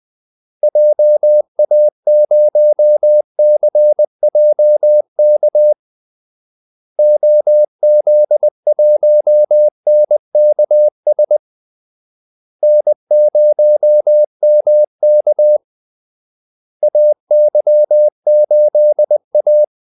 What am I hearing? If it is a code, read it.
Morse code, heard as a keyed tone, about 20 wpm, 600 Hz. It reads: JA0CJK OZ1NKS N0MK AY8A
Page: Kaldesignaler 20 wpm